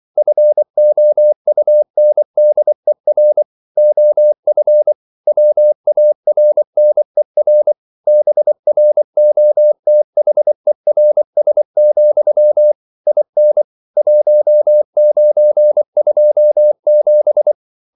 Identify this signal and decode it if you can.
Morse code, heard as a keyed tone, about 24 wpm, 600 Hz. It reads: FOUNDER OF WARNER BROTHERS, IN 1927
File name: CW / Practice / EnglishQuotations0424